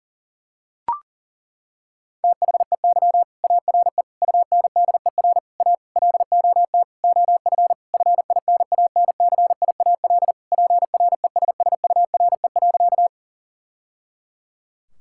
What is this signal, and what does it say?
Morse code, heard as a keyed tone, about 37 wpm, 700 Hz. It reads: THEY ARE UNDER A LOT OF FINANCIAL PRESSURE.
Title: Sentences from Top 1000 Words - Review Part 6 - 40wpm